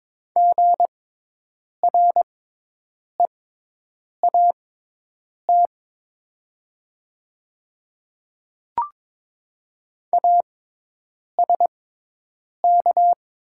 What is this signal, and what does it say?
Morse code, heard as a keyed tone, about 22 wpm, spaced as for 8 wpm, 700 Hz. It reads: GREAT ASK
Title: 6x Character Spacing - Top 200 Words - 22wpm